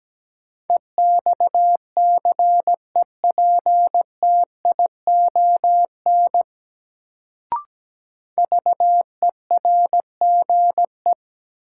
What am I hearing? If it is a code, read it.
Morse code, heard as a keyed tone, about 17 wpm, 700 Hz. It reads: EXCEPTION VERGE
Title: Sets of 4 Words - New Word Drill - 17wpm